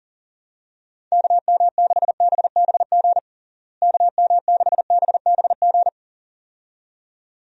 Morse code, heard as a keyed tone, about 40 wpm, 700 Hz. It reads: KM6BBG KM6BBG
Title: USA Call Signs - 6 Characters - Repeated 2x - 40wpm